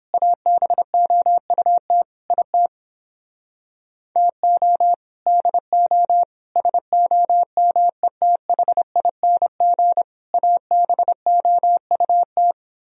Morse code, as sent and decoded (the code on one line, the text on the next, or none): .- -... --- ..- - / .. - / - --- / -.. --- / ... --- -- . - .... .. -. --. / .- -... --- ..- -
ABOUT IT TO DO SOMETHING ABOUT